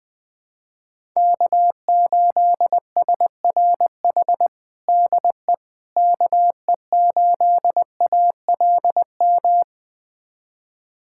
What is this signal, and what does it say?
Morse code, heard as a keyed tone, about 21 wpm, 700 Hz. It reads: K8SRH DE KE8ALM
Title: USA Callsign DE Callsign 20wpm